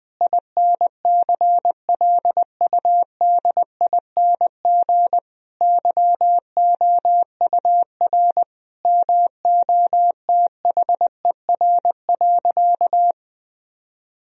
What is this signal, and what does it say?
Morse code, heard as a keyed tone, about 20 wpm, 700 Hz. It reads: INCLUDING YOUR MOTHER.
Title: Sentences from Top 1000 Words - Review Part 4 - 20wpm